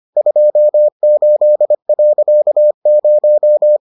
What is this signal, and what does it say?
Morse code, heard as a keyed tone, about 25 wpm, 600 Hz. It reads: 28.0